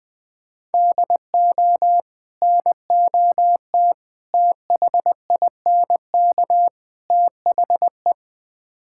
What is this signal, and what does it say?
Morse code, heard as a keyed tone, about 20 wpm, 700 Hz. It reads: DO NOT THINK THE
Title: Build A Phrase - 4-Word Phrases - 20wpm